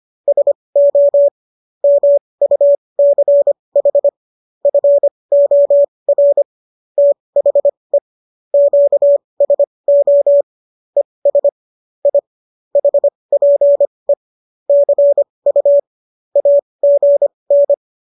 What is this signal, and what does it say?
Morse code, heard as a keyed tone, about 25 wpm, spaced as for 20 wpm, 570 Hz. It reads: SO MUCH FOR THE QSO ES I HPE CU AGN